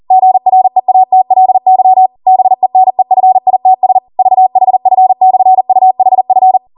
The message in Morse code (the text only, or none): GREATLY BENEFITS VHF/UHF